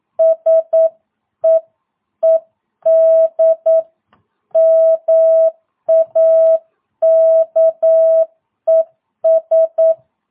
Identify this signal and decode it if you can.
Morse code, heard as a keyed tone, about 9 wpm, 650 Hz. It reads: SEED MAKES
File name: jh8cbh / 2022cw